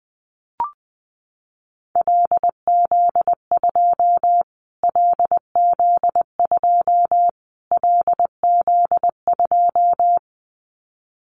Morse code, heard as a keyed tone, about 19 wpm, 700 Hz. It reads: LZ2 LZ2 LZ2
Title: Callsign Prefixes for Top 25 DXCC - Repeated 3x - 20wpm